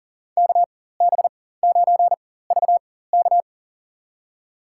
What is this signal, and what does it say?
Morse code, heard as a keyed tone, about 40 wpm, spaced as for 19 wpm, 700 Hz. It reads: KB9VK
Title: USA Call Signs - 5 Characters - 4x Character Spacing - 40wpm